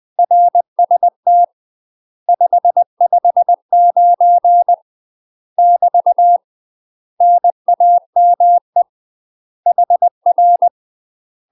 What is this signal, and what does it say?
Morse code, heard as a keyed tone, about 20 wpm, 700 Hz. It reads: RST 559 <BT> NAME HR